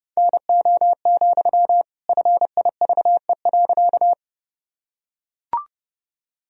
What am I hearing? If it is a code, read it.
Morse code, heard as a keyed tone, about 30 wpm, 700 Hz. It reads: NO, FIVE.
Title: Sentences from Top 1000 Words - Review Part 2 - 30wpm